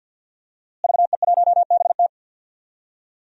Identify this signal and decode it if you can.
Morse code, heard as a keyed tone, about 53 wpm, 700 Hz. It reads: VE1BT
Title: DXCC-100 Call Signs - 5 Characters - Repeated 2x - 50wpm